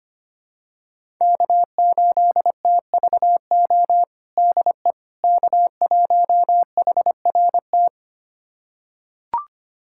Morse code, heard as a keyed tone, about 25 wpm, 700 Hz. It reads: K8TVO DE K1HRT
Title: USA Callsign DE Callsign - 5 Characters - 25wpm